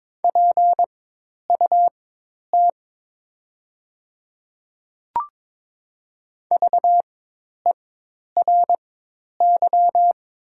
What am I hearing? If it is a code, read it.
Morse code, heard as a keyed tone, about 21 wpm, spaced as for 10 wpm, 700 Hz. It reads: PUT VERY
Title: 4x Character Spacing - Top 200 Words - 22wpm